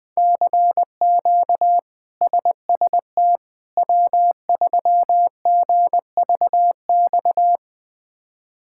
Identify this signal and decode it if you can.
Morse code, heard as a keyed tone, about 20 wpm, 700 Hz. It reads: CQ SST W3GVX
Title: K1USN SST QSO Practice - 20wpm